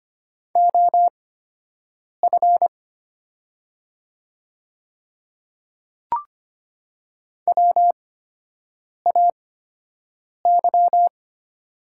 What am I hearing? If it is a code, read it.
Morse code, heard as a keyed tone, about 25 wpm, spaced as for 7 wpm, 700 Hz. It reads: OF WAY
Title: 8x Character Spacing - Top 100 Words - 25wpm